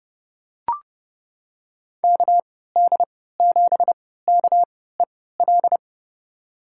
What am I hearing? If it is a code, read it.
Morse code, heard as a keyed tone, about 31 wpm, spaced as for 17 wpm, 700 Hz. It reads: KD7KEL
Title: USA Call Signs - 3x Character Spacing - 30wpm